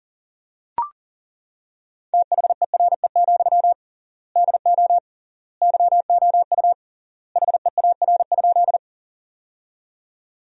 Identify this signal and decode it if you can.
Morse code, heard as a keyed tone, about 40 wpm, 700 Hz. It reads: THERE, DO YOU HEAR?
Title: 3x Word-Spacing - Sentences from Top 300 Words - 40wpm